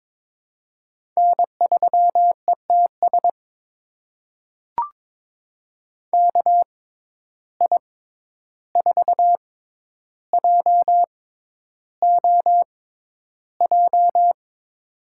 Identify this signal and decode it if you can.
Morse code, heard as a keyed tone, about 22 wpm, spaced as for 11 wpm, 700 Hz. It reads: N3ETS KI4JOJ
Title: USA Call Signs - 6x Character Spacing - 22wpm